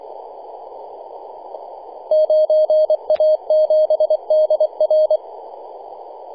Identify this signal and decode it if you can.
Morse code, heard as a keyed tone, about 23 wpm, 620 Hz. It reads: 9A7DR
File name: e51dwc-21-010mhz-cw